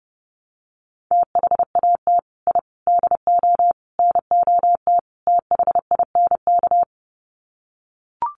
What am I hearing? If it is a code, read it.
Morse code, heard as a keyed tone, about 30 wpm, 700 Hz. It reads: THAT I DO NOT THINK
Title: Sets of 5 Words Encore 30wpm